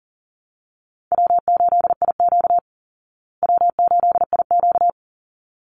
Morse code, heard as a keyed tone, about 40 wpm, 700 Hz. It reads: W8IQ W8IQ
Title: Easy Call Signs Repeated 2x - 40wpm